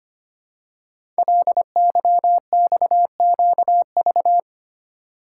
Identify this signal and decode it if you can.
Morse code, heard as a keyed tone, about 25 wpm, 700 Hz. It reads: LYXQV
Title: Code Groups - 5 Letters - 25wpm